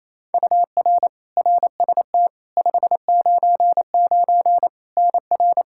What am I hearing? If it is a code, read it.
Morse code, heard as a keyed tone, about 28 wpm, 700 Hz. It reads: UR RST 599 NR
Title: Practice QSO - Exchange 1 - Beginning - 28wpm